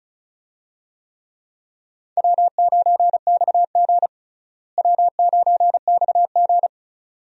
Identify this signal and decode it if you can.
Morse code, heard as a keyed tone, about 34 wpm, 700 Hz. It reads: W9XG W9XG
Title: USA Call Signs - 4 Characters - Repeated 2x - 35wpm